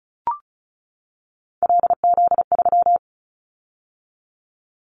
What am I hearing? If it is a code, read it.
Morse code, heard as a keyed tone, about 36 wpm, 690 Hz. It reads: LZ3
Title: Callsign Prefixes for Top 25 DXCC - 35wpm